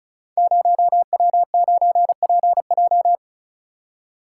Morse code, heard as a keyed tone, about 36 wpm, 700 Hz. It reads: 0W9PJ
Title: Code Groups - 5 Letter:Numbers - 35wpm